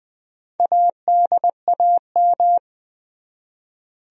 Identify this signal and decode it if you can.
Morse code, heard as a keyed tone, about 20 wpm, 700 Hz. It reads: ADAM
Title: USA Common Names - 20wpm